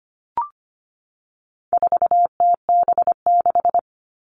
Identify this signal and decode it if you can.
Morse code, heard as a keyed tone, about 25 wpm, 700 Hz. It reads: 4TB6
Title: All Cut Numbers - 4-Digit 25wpm